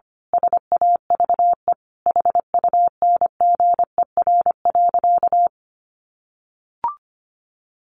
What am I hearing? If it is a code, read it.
Morse code, heard as a keyed tone, about 25 wpm, 700 Hz. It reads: SAVE HUNGER.
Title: A Sound Of Thunder 25wpm